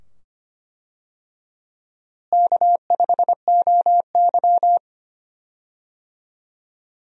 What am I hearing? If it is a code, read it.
Morse code, heard as a keyed tone, about 25 wpm, 700 Hz. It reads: K5OY
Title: USA Call Signs - 4 Characters - Repeated 2x - 25wpm